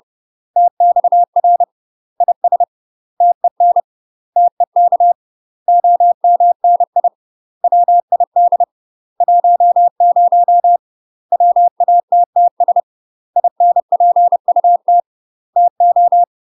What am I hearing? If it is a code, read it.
Morse code, heard as a keyed tone, about 30 wpm, 700 Hz. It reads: TXR IS TEN TEK OMNI WID 10 WATTS INPUT TO
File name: UK MASTER QSOs TRAINING / 30WPM / UK MASTER QSOs 0030